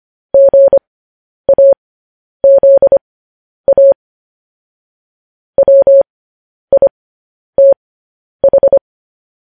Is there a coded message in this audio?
Yes: Morse code from 0.3 to 8.8 s, heard as a keyed tone, about 25 wpm, spaced as for 10 wpm, 550 Hz. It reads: GAZA WITH